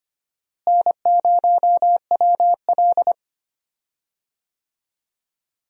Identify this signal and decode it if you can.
Morse code, heard as a keyed tone, about 25 wpm, 700 Hz. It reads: N0WL
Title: USA Call Signs - 4 Characters - Repeated 2x - 25wpm